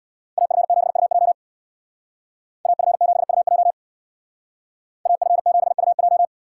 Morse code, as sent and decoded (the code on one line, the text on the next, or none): .- ..-. --... ..- .--- / .- ..-. --... ..- .--- / .- ..-. --... ..- .---
AF7UJ AF7UJ AF7UJ